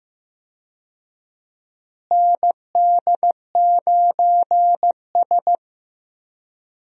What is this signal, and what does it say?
Morse code, heard as a keyed tone, about 15 wpm, 700 Hz. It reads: ND9S
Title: USA Call Signs - 4 Characters - Repeated 3x - 15wpm